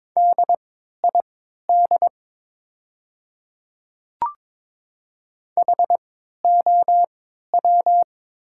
Morse code, heard as a keyed tone, about 22 wpm, spaced as for 13 wpm, 700 Hz. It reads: DID HOW
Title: 3x Character Spacing - Top 100 Words - 22wpm